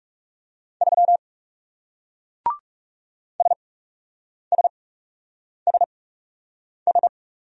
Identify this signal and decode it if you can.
Morse code, heard as a keyed tone, about 40 wpm, spaced as for 8 wpm, 700 Hz. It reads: 3 SSSS